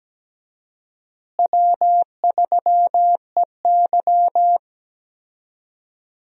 Morse code, heard as a keyed tone, about 17 wpm, 700 Hz. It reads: W3EY
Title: USA Call Signs - 4 Characters - Repeated 2x - 17wpm